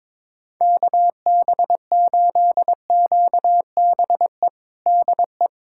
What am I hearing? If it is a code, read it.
Morse code, heard as a keyed tone, about 22 wpm, 700 Hz. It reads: KB8QBE DE
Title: USA Callsign DE Callsign 22wpm